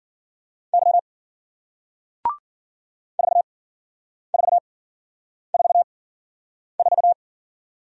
Morse code, heard as a keyed tone, about 53 wpm, spaced as for 9 wpm, 700 Hz. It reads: Y 4444